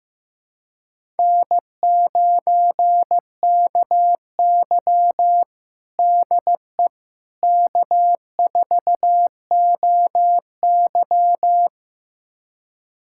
Morse code, heard as a keyed tone, about 16 wpm, 700 Hz. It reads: N9KY DE K4OY
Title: USA Callsign DE Callsign - 4 Characters - 15wpm